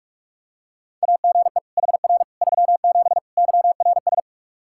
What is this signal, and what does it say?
Morse code, heard as a keyed tone, about 46 wpm, 700 Hz. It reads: AGE HR 37 YRS
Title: Practice QSO - Exchange 3 - More Info - 45wpm